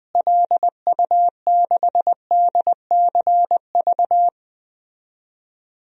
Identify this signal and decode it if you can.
Morse code, heard as a keyed tone, about 20 wpm, 700 Hz. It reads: LU6DCV
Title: DXCC-100 Call Signs - 6 Characters - Repeated 2x - 20wpm